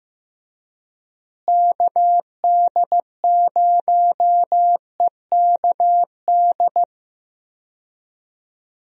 Morse code, heard as a keyed tone, about 16 wpm, 700 Hz. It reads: KD0EKD
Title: USA Call Signs - 6 Characters - Repeated 3x - 15wpm